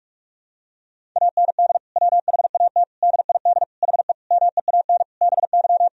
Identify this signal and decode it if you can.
Morse code, heard as a keyed tone, about 46 wpm, 700 Hz. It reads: AND WHAT DID HE MEAN BY
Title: Sentences from Top 1000 Words - Review Part 3 - 45wpm